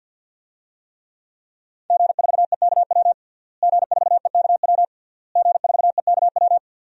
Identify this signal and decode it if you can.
Morse code, heard as a keyed tone, about 46 wpm, 700 Hz. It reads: G4EKW G4EKW G4EKW
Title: DXCC-100 Call Signs - 5 Characters - Repeated 3x - 50wpm